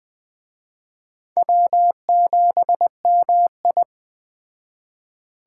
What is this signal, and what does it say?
Morse code, heard as a keyed tone, about 21 wpm, 700 Hz. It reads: W7MI